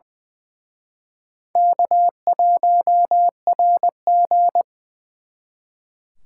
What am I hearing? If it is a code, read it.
Morse code, heard as a keyed tone, about 24 wpm, 700 Hz. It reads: K1RG
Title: Speed Racing - USA Call Signs 20wpm